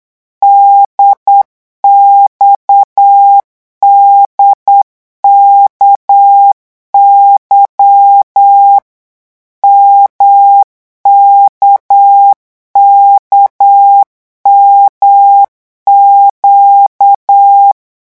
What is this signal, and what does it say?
Morse code, heard as a keyed tone, about 8 wpm, 800 Hz. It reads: DXDKY MKKMQ
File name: morse mp3 / revision3-vitesse 8 mots